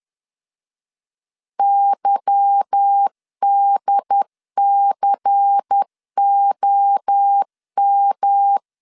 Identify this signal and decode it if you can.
Morse code, heard as a keyed tone, about 10 wpm, 800 Hz. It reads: YDCOM
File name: CW / DET / 1 9 1500C S A 10wpm 4511